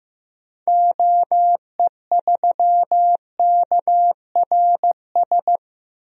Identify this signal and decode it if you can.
Morse code, heard as a keyed tone, about 15 wpm, 700 Hz. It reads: OE3KRS
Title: DXCC-100 Call Signs - 6 Characters - Repeated 2x - 15wpm